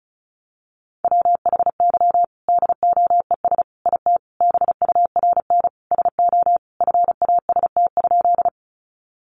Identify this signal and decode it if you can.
Morse code, heard as a keyed tone, about 34 wpm, 700 Hz. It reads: WHY DOES IT BURN SO FAST?
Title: Sentences from Top 1300 Words 35wpm